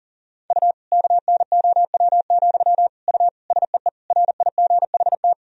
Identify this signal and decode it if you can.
Morse code, heard as a keyed tone, about 40 wpm, 700 Hz. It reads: U KNOW, U SEE RIGHT
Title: Sentences from Top 200 Words and Abbreviations 40wpm